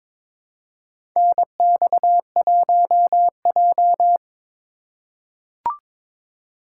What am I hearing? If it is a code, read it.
Morse code, heard as a keyed tone, about 21 wpm, 700 Hz. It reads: NX1J
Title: USA Call Signs - 4 Characters - 3x Character Spacing - 22wpm